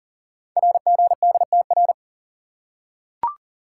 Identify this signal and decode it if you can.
Morse code, heard as a keyed tone, about 42 wpm, 690 Hz. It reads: RGDTR
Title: Code Groups - 5 Letters - 40wpm